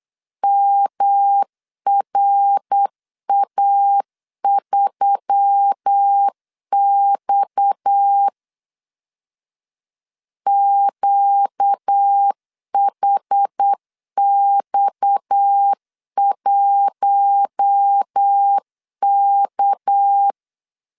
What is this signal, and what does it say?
Morse code, heard as a keyed tone, about 8 wpm, 800 Hz. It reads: MRA3X QHX1K
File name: LEZIONI / 6 PKX0 / 6 PKX 8wpm 3041